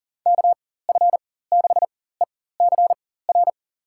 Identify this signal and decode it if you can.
Morse code, heard as a keyed tone, about 40 wpm, spaced as for 19 wpm, 700 Hz. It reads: KF6ECR
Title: USA Call Signs - 6 Characters - 4x Character Spacing - 40wpm